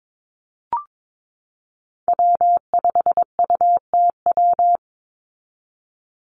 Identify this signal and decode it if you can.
Morse code, heard as a keyed tone, about 22 wpm, 700 Hz. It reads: W5UTW
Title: Call Signs Repeated 2x - 22wpm